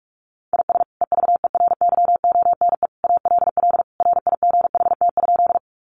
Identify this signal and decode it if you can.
Morse code, heard as a keyed tone, about 44 wpm, 700 Hz. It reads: IS EVERYONE ALL RIGHT?
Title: Sentences from Top 800 Words 45wpm